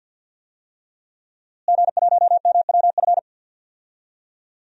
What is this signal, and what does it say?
Morse code, heard as a keyed tone, about 52 wpm, 700 Hz. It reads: G1MWF
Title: DXCC-100 Call Signs - 5 Characters - Repeated 3x - 50wpm